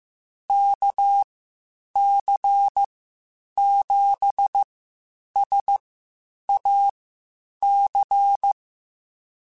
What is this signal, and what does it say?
Morse code, heard as a keyed tone, about 15 wpm, spaced as for 8 wpm, 800 Hz. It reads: KC7SAC